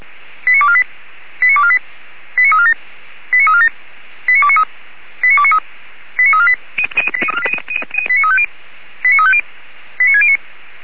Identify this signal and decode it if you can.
Five-tone selective call (ZVEI), heard as quick runs of five tones, tones of about 70 ms. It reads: 89248 89248 89347 89347 89293 89293 89348 89350 89609
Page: /464/SDR/ZVEI/